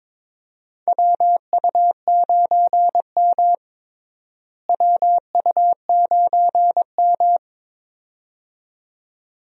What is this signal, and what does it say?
Morse code, heard as a keyed tone, about 21 wpm, 700 Hz. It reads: WU9M WU9M
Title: USA Call Signs - 4 Characters - Repeated 2x - 22wpm